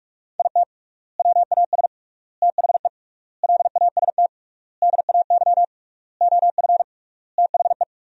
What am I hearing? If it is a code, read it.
Morse code, heard as a keyed tone, about 42 wpm, 700 Hz. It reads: IT WAS THE LAST DAY OF THE
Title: 3x Word-Spacing - Sentences from Top 300 Words - 45wpm